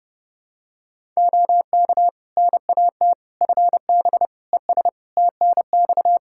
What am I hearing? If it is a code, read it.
Morse code, heard as a keyed tone, about 30 wpm, 700 Hz. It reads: OK NAT FB ES TNX
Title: Practice QSO - Exchange 2 - Info - 30wpm